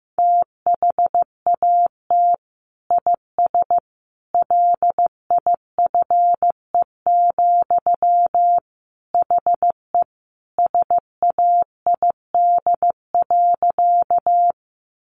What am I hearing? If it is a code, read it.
Morse code, heard as a keyed tone, about 15 wpm, 700 Hz. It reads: THAT IS LIFE, HE SAID.
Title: Sentences from Top 900 Words 15wpm